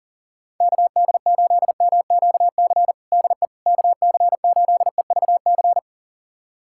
Morse code, heard as a keyed tone, about 37 wpm, 700 Hz. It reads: KD8MQC DE KC8EVC
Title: USA Callsign DE Callsign 40wpm